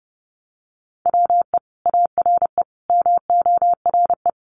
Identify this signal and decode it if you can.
Morse code, heard as a keyed tone, about 30 wpm, 700 Hz. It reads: WE ARE MORE
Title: Sentences from Top 500 Words 30wpm